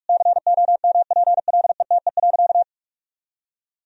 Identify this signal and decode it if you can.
Morse code, heard as a keyed tone, about 46 wpm, 700 Hz. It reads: COMPLETE.
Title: Sentences from Top 1000 Words - Review Part 2 - 45wpm